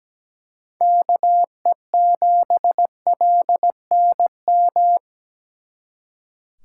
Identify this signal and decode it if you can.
Morse code, heard as a keyed tone, about 20 wpm, 700 Hz. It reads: KE7LNM
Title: Speed Racing - USA 2x3 Call Signs 17wpm